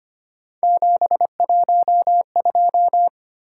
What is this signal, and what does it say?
Morse code, heard as a keyed tone, about 25 wpm, 700 Hz. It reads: 712
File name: Three Letter-Number Combinations - 25wpm